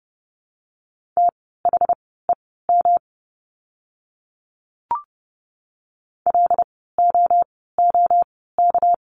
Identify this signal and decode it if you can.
Morse code, heard as a keyed tone, about 30 wpm, spaced as for 17 wpm, 700 Hz. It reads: THEM LOOK
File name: course / 30:10wpm / 237 Lesson 38f - Letter q - Words 30wpm